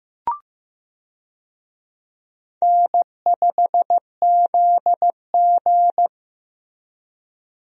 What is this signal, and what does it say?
Morse code, heard as a keyed tone, about 15 wpm, 700 Hz. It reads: N5ZG
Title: USA Call Signs - 4 Characters - Repeated 2x - 15wpm